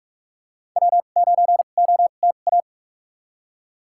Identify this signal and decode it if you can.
Morse code, heard as a keyed tone, about 46 wpm, spaced as for 32 wpm, 700 Hz. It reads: W9OTA
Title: USA Call Signs - 5 Characters - 2x Character Spacing - 45wpm